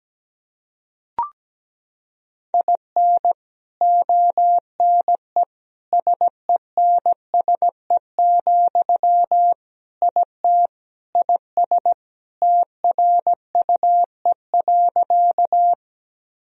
Morse code, heard as a keyed tone, about 17 wpm, 700 Hz. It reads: IN ONE SENSE, IT IS TRUE.
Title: Sentences from Top 1000 Words - Review Part 4 - 17wpm